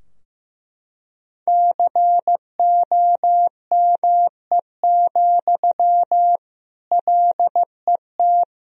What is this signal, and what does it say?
Morse code, heard as a keyed tone, about 16 wpm, 700 Hz. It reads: COME, LET
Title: Sentences from Top 1000 Words - Review Part 2 - 15wpm